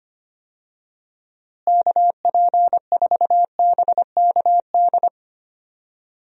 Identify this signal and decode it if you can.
Morse code, heard as a keyed tone, about 25 wpm, 700 Hz. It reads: KP4BKD
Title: DXCC-100 Call Signs - 6 Characters - Repeated 3x - 25wpm